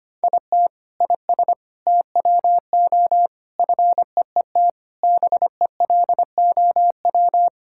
Morse code, heard as a keyed tone, about 25 wpm, 700 Hz. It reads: IT IS TWO FEET BELOW
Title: Sentences from Top 1000 Words - Review Part 5 - 25wpm